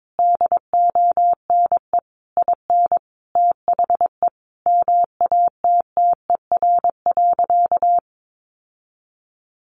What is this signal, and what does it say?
Morse code, heard as a keyed tone, about 22 wpm, 700 Hz. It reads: DONE IN THE MATTER.
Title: Sentences from Top 700 Words 22wpm